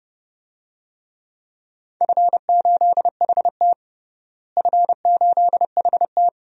Morse code, heard as a keyed tone, about 32 wpm, 700 Hz. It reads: F8HT F8HT
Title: DXCC-100 Call Signs - 4 Characters - Repeated 2x - 30wpm